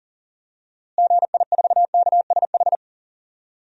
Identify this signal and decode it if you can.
Morse code, heard as a keyed tone, about 40 wpm, 700 Hz. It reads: GI4KSH
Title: DXCC-100 Call Signs - 6 Characters - Repeated 2x - 40wpm